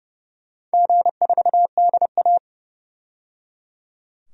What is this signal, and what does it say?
Morse code, heard as a keyed tone, about 29 wpm, 700 Hz. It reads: G4DA
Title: DXCC-100 Call Signs - 4 Characters - 30wpm